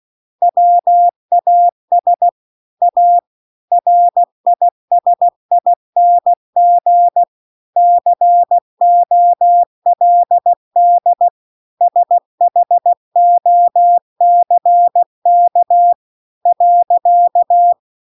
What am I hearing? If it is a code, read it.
Morse code, heard as a keyed tone, about 16 wpm, 700 Hz. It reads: WAS A RISING COLD SHOCK .